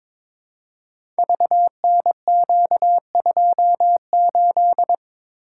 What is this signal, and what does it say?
Morse code, heard as a keyed tone, about 23 wpm, 700 Hz. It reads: VNQ28
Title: Code Groups - 5 Letter:Numbers - 22wpm